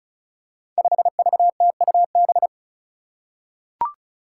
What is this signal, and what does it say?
Morse code, heard as a keyed tone, about 36 wpm, 700 Hz. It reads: 5VTUB
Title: Five Challenging Letter-Number Combinations 35wpm